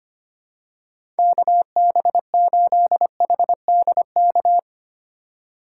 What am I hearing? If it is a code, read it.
Morse code, heard as a keyed tone, about 25 wpm, 700 Hz. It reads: KB8HDK
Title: USA Call Signs - 6 Characters - 3x Character Spacing - 25wpm